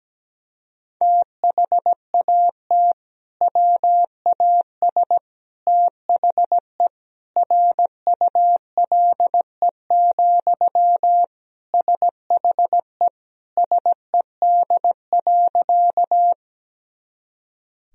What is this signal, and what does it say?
Morse code, heard as a keyed tone, about 17 wpm, 700 Hz. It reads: THAT WAS THE RULE, SHE SED.
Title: Sentences from Top 400 Words and Abbreviations 17wpm